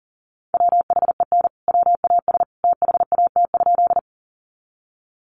Morse code, heard as a keyed tone, about 40 wpm, 700 Hz. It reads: WHEN WAS THAT?
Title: Sentences from Top 100 Words 40wpm